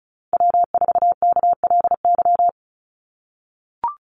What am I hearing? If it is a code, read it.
Morse code, heard as a keyed tone, about 36 wpm, 700 Hz. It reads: W4KLY
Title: Call Signs - 35wpm